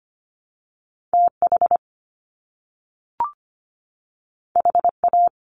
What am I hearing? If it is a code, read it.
Morse code, heard as a keyed tone, about 25 wpm, 700 Hz. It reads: TH HA
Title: Top Two Letter Combinations in Words 25wpm